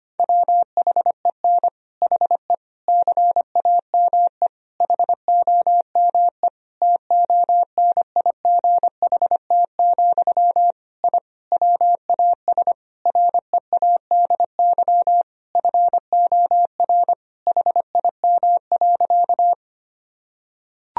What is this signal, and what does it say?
Morse code, heard as a keyed tone, about 25 wpm, 700 Hz. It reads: WHEN HE CAME HOME TONIGHT, I WAS READY FOR HIM.
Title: Sentences from Top 1000 Words - Review Part 3 - 25wpm